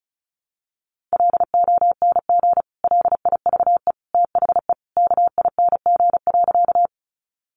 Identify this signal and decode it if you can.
Morse code, heard as a keyed tone, about 34 wpm, 700 Hz. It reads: LONG LIVE THE KING.
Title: Sentences from Top 400 Words 35wpm